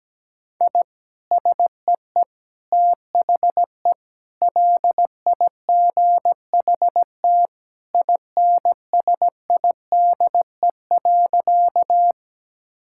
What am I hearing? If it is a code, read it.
Morse code, heard as a keyed tone, about 17 wpm, 700 Hz. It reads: I SEE THE LIGHT INSIDE.
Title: Sentences from Top 1000 Words - Review Part 6 - 17wpm